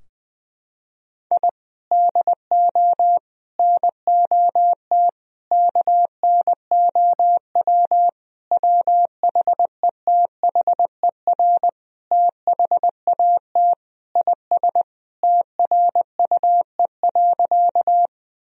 Morse code, heard as a keyed tone, about 20 wpm, 700 Hz. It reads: I DO NOT KNOW WHETHER THAT IS TRUE.
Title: Sentences from Top 1000 Words - Review Part 3 - 20wpm